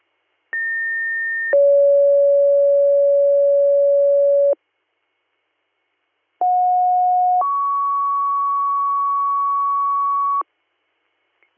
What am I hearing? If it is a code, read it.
Two-tone page, heard as a one-second tone, then a three-second tone, about 1800 then 560 Hz, about 730 then 1120 Hz. It is heard from 0.5 to 10.4 s.